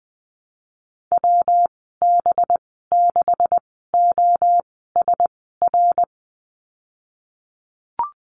Morse code, heard as a keyed tone, about 20 wpm, spaced as for 15 wpm, 700 Hz. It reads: WB6OSR
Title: 186 Lesson 30g - Letter b - Call Signs 20wpm